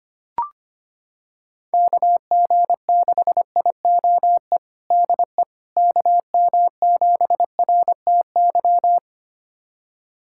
Morse code, heard as a keyed tone, about 25 wpm, 700 Hz. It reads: KG6IOE DE KM7RTY
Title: USA Callsign DE Callsign 25wpm